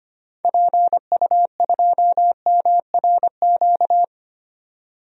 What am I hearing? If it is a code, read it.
Morse code, heard as a keyed tone, about 25 wpm, 700 Hz. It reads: PU2MRQ
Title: DXCC-100 Call Signs - 6 Characters - 25wpm